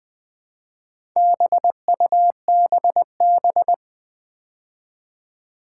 Morse code, heard as a keyed tone, about 20 wpm, 700 Hz. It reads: BUBB
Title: Build A Word - 2 Letters At A Time - 6-Letter Words - 20wpm